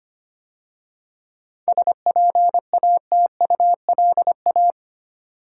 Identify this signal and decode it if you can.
Morse code, heard as a keyed tone, about 31 wpm, 700 Hz. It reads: SPATULA
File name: Easy Words - Speed-Racing with Hints - 25wpm